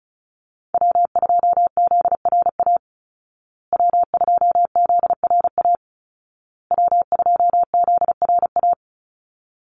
Morse code, heard as a keyed tone, about 34 wpm, 700 Hz. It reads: W2ZRA W2ZRA W2ZRA
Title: Call Signs Repeated 2x - 35wpm